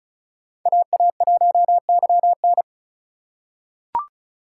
Morse code, heard as a keyed tone, about 36 wpm, 700 Hz. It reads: AA1YN
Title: USA Call Signs - 5 Characters - 2x Character Spacing - 35wpm